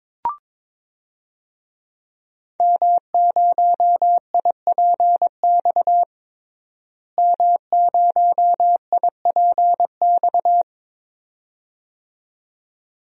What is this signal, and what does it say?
Morse code, heard as a keyed tone, about 22 wpm, 700 Hz. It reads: M0IPX M0IPX
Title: DXCC-100 Call Signs - 5 Characters - Repeated 2x - 22wpm